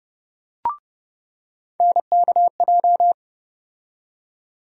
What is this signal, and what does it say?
Morse code, heard as a keyed tone, about 30 wpm, 700 Hz. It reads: NKJ
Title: USA Call Sign Postfixes - 3 Letters 30wpm